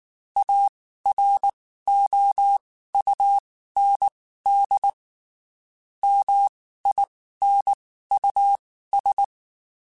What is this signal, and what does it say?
Morse code, heard as a keyed tone, about 19 wpm, spaced as for 14 wpm, 800 Hz. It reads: AROUND MINUS